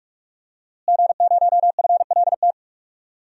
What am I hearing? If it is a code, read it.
Morse code, heard as a keyed tone, about 42 wpm, 700 Hz. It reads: G0FLT
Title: DXCC-100 Call Signs - 5 Characters - Repeated 2x - 45wpm